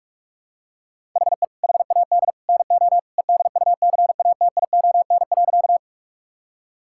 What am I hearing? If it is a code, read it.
Morse code, heard as a keyed tone, about 41 wpm, 700 Hz. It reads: HE HAD NO EDUCATION.
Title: Sentences from Top 1000 Words - Review Part 6 - 45wpm